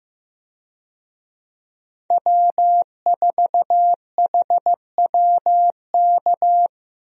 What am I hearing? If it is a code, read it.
Morse code, heard as a keyed tone, about 15 wpm, 700 Hz. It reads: W4HWK
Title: USA Call Signs - 5 Characters - Repeated 2x - 15wpm